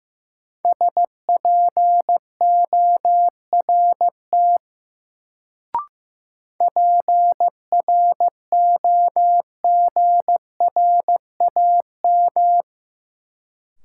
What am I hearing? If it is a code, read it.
Morse code, heard as a keyed tone, about 15 wpm, 700 Hz. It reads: SPORT PROGRAM
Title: Sentences from Top 1500 Words - New Word Drill Alt - 15wpm